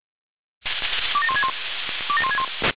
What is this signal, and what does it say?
Five-tone selective call (ZVEI), heard as quick runs of five tones, tones of about 70 ms. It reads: 27161 27161